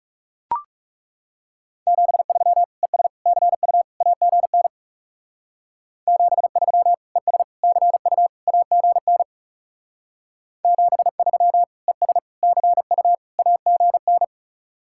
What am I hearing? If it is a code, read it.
Morse code, heard as a keyed tone, about 36 wpm, 700 Hz. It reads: 73 ES CU AGN 73 ES CU AGN 73 ES CU AGN
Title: Speed-Racing Common QSO Phrases - 30wpm